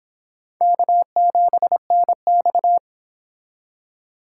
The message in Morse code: K7NX

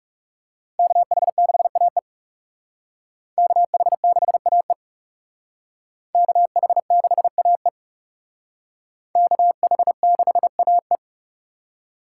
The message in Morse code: KH6AE KH6AE KH6AE KH6AE

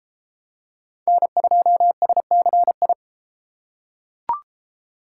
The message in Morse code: N2SCI

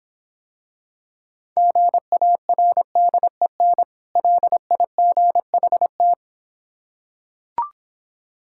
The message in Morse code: GARDEN LIGHT

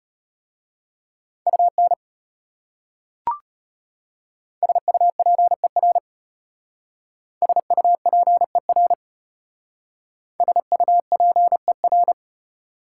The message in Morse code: UN SUPER SUPER SUPER

